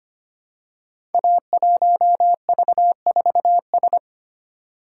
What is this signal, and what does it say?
Morse code, heard as a keyed tone, about 25 wpm, 700 Hz. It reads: A1V4S
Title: Code Groups - 5 Letter:Numbers - 25wpm